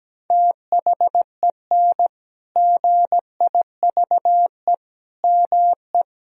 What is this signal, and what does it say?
Morse code, heard as a keyed tone, about 17 wpm, 700 Hz. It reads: THEN GIVE ME
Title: Sentences from Top 1000 Words - Review Part 1 - 17wpm